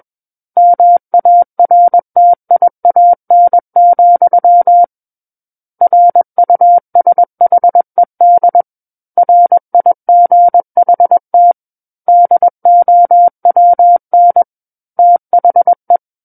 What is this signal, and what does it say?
Morse code, heard as a keyed tone, about 21 wpm, 700 Hz. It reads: MARTIAN, RUSHED RIGHT DOWN THE
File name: Book One / 12-Chapter 12 - 21 WPM